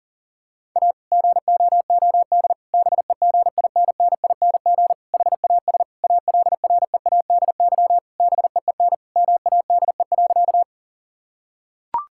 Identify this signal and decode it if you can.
Morse code, heard as a keyed tone, about 40 wpm, 700 Hz. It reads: A GOOD BEGINNING HAS ALREADY BEEN MADE.
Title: Sentences from Top 1000 Words - Review Part 5 - 40wpm